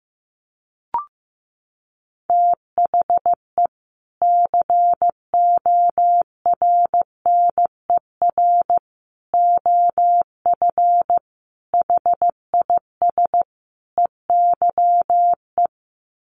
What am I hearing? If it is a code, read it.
Morse code, heard as a keyed tone, about 15 wpm, 700 Hz. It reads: THE CORNER OF HIS EYE
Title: Sets of 5 Words Encore2 15wpm